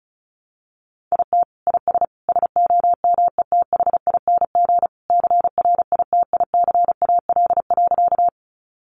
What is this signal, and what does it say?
Morse code, heard as a keyed tone, about 34 wpm, 700 Hz. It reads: IT IS SOMETHING CRITICAL.
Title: Sentences from Top 1100 Words 35wpm